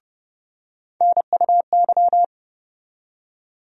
Morse code, heard as a keyed tone, about 30 wpm, 700 Hz. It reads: NUY
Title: USA Call Sign Postfixes - 3 Letters 30wpm